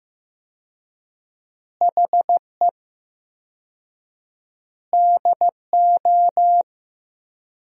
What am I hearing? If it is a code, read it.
Morse code, heard as a keyed tone, about 16 wpm, 700 Hz. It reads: HE DO
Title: 4x Word-Spacing - Sentences from Top 100 Words - 15wpm